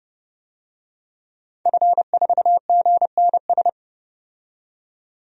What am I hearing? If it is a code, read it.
Morse code, heard as a keyed tone, about 31 wpm, 700 Hz. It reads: F4GNS
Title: DXCC-100 Call Signs - 5 Characters - 30wpm